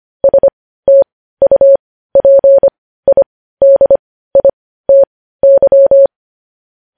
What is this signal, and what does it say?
Morse code, heard as a keyed tone, about 25 wpm, spaced as for 15 wpm, 550 Hz. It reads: STUPIDITY
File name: podcast / News headlines-1x-25x15